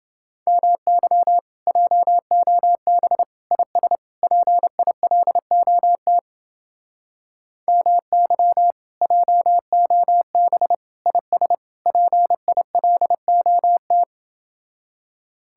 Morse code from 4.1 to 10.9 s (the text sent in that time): PILOT MY JOB